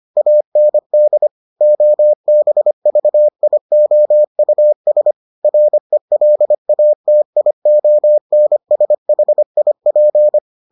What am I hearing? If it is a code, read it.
Morse code, heard as a keyed tone, about 25 wpm, 600 Hz. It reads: AND OBVIOUS RELATIONSHIP